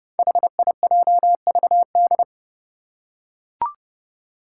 Morse code, heard as a keyed tone, about 30 wpm, 700 Hz. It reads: HIJVD
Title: Five Challenging Letter-Number Combinations 30wpm